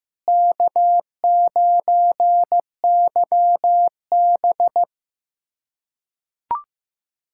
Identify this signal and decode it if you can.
Morse code, heard as a keyed tone, about 15 wpm, 700 Hz. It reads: K9YB
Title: USA Callsign DE Callsign - 4 Characters - 15wpm